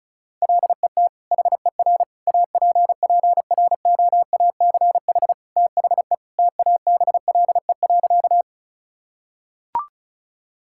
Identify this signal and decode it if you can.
Morse code, heard as a keyed tone, about 36 wpm, 700 Hz. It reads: LET HER APPROACH THE TABLE.
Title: Sentences from Top 1000 Words - Review Part 6 - 35wpm